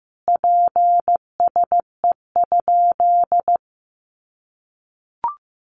Morse code, heard as a keyed tone, about 14 wpm, 700 Hz. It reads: PSE?
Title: Common QSO Phrases 15wpm